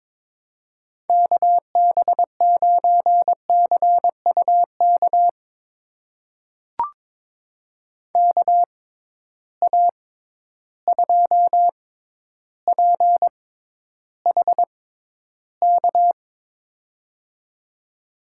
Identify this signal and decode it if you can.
Morse code, heard as a keyed tone, about 22 wpm, spaced as for 11 wpm, 700 Hz. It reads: KB9CUK KA2PHK
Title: USA Call Signs - 6 Characters - 6x Character Spacing - 22wpm